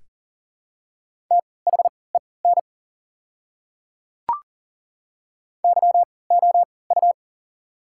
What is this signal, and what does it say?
Morse code, heard as a keyed tone, about 40 wpm, spaced as for 23 wpm, 700 Hz. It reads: THEN YOU
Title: 3x Character Spacing - Top 100 Words - 40wpm